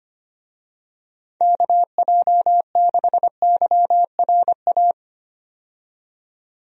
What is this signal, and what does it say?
Morse code, heard as a keyed tone, about 25 wpm, 700 Hz. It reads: KJ6YRA
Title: USA Call Signs - 6 Characters - 25wpm